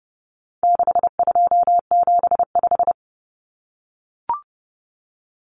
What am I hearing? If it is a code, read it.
Morse code, heard as a keyed tone, about 30 wpm, 700 Hz. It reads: 6275
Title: Common Cut Numbers - 4-Digit 30wpm